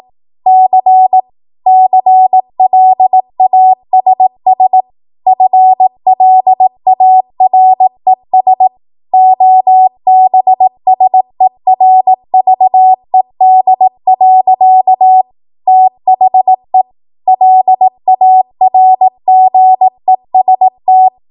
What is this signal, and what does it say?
Morse code, heard as a keyed tone, about 18 wpm, 750 Hz. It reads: C CLASS FLARES OBSERVED. THE LARGEST
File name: Morse / 18WPM